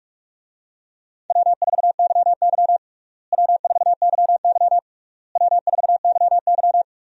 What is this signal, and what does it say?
Morse code, heard as a keyed tone, about 43 wpm, 700 Hz. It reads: W4YY W4YY W4YY
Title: USA Call Signs - 4 Characters - Repeated 3x - 45wpm